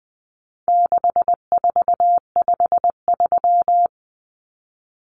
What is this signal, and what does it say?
Morse code, heard as a keyed tone, about 20 wpm, 700 Hz. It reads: 6453
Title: Numbers - 4-Digit 20wpm